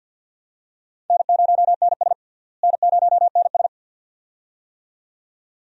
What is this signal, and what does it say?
Morse code, heard as a keyed tone, about 49 wpm, 700 Hz. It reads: N0NS N0NS
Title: USA Call Signs - 4 Characters - Repeated 2x - 50wpm